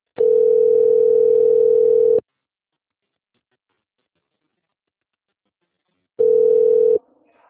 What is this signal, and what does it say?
Ringback tone (North America), 440 + 480 Hz.
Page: Itapetininga Remoto - SP